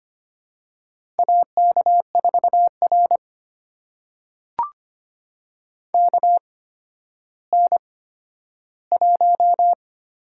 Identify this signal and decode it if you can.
Morse code, heard as a keyed tone, about 25 wpm, spaced as for 12 wpm, 700 Hz. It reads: AK4R KN1
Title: USA Call Signs - 4 Characters - 8x Character Spacing - 25wpm